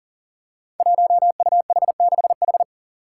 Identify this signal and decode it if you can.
Morse code, heard as a keyed tone, about 39 wpm, 700 Hz. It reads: 1UH6H
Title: Five Challenging Letter-Number Combinations 40wpm